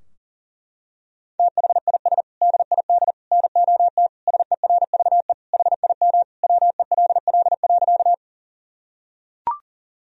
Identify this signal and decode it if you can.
Morse code, heard as a keyed tone, about 40 wpm, 700 Hz. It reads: THIS DID NOT SERVE HIM WELL.
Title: Sentences from Top 1000 Words - Review Part 2 - 40wpm